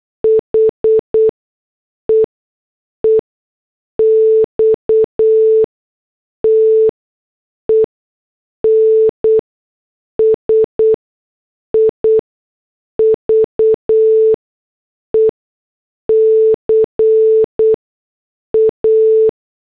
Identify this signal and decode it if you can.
Morse code, heard as a keyed tone, about 8 wpm, spaced as for 6 wpm, 440 Hz. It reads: HE EXTENSIVE CA
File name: morse / headlines